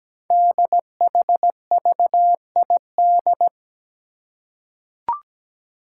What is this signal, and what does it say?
Morse code, heard as a keyed tone, about 17 wpm, 700 Hz. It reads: DHVID
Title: Code Groups - 5 Letters - 17wpm